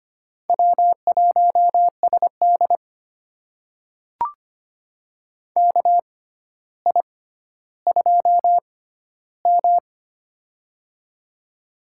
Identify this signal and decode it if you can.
Morse code, heard as a keyed tone, about 25 wpm, spaced as for 13 wpm, 700 Hz. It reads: W1SD KI2M
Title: USA Call Signs - 6x Character Spacing - 25wpm